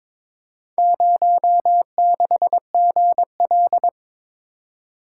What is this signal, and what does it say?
Morse code, heard as a keyed tone, about 22 wpm, 700 Hz. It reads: 06GL
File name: practice / 22wpm / Four Letter-Number Combinations - 22wpm